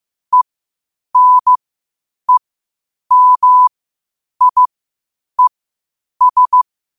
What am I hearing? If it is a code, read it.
Morse code, heard as a keyed tone, about 15 wpm, spaced as for 8 wpm, 1000 Hz. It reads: ENEMIES
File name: qotdfiles / 8wpm